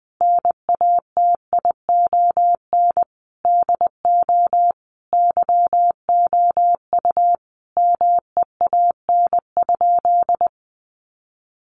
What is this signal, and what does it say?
Morse code, heard as a keyed tone, about 20 wpm, 700 Hz. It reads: NATION DO YOU MEAN?
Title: Sentences from Top 900 Words 20wpm